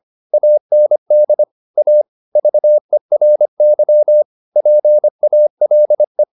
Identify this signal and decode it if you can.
Morse code, heard as a keyed tone, about 25 wpm, 600 Hz. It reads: AND A VERY PALE